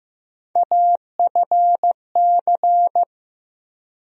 Morse code, heard as a keyed tone, about 15 wpm, 700 Hz. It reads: AFC
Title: USA Call Sign Postfixes - 3 Letters 15wpm